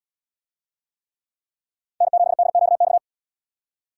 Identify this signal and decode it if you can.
Morse code, heard as a keyed tone, about 71 wpm, 700 Hz. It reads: N7DQX